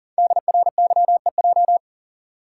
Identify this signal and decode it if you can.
Morse code, heard as a keyed tone, about 42 wpm, 690 Hz. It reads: DRYEJ
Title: Code Groups - 5 Letters - 40wpm